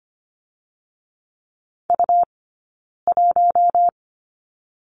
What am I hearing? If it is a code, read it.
Morse code, heard as a keyed tone, about 25 wpm, 700 Hz. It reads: U 1
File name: course / 25:10wpm / 099 Lesson 17d - Letter u - Two Characters 25wpm